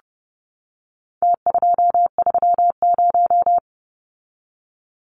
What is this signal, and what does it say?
Morse code, heard as a keyed tone, about 30 wpm, 700 Hz. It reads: T230
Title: Numbers - Contest Style - 4-Digit 30wpm